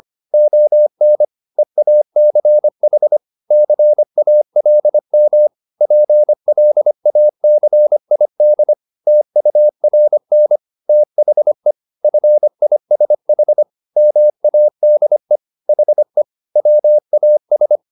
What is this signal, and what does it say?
Morse code, heard as a keyed tone, about 25 wpm, 600 Hz. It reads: ON EACH CALM PLACID TURN THE FISH MADE HE WAS